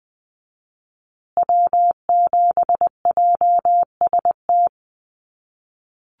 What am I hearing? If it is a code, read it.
Morse code, heard as a keyed tone, about 20 wpm, 700 Hz. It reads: W7JST
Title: USA Call Signs Repeated 2x 20wpm